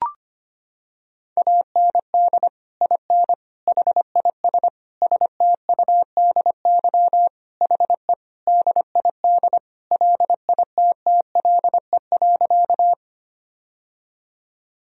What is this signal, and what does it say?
Morse code, heard as a keyed tone, about 25 wpm, 700 Hz. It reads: AND IN HIS STUDY HE DID LITTLE.
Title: Sentences from Top 1000 Words - Review Part 1 - 25wpm